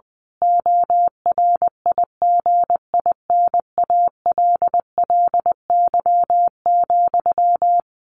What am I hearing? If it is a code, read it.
Morse code, heard as a keyed tone, about 20 wpm, 700 Hz. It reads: ORIGINALLY,
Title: A Sound Of Thunder 20wpm